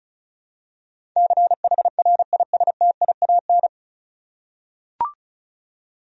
Morse code, heard as a keyed tone, about 36 wpm, 700 Hz. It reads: CHRISTIAN
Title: USA Common Names - 35wpm